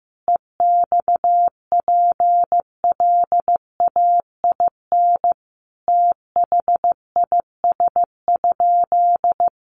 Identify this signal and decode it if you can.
Morse code, heard as a keyed tone, about 15 wpm, 700 Hz. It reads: EXPLAIN THIS?
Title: Sentences from Top 1100 Words 15wpm